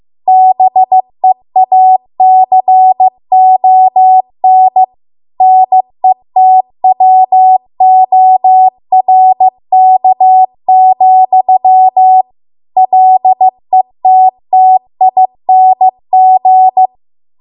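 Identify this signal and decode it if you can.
Morse code, heard as a keyed tone, about 15 wpm, 750 Hz. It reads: BEACON NETWORK, LETTING
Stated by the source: Morse code practice at 15 wpm from the national amateur radio association's code archive.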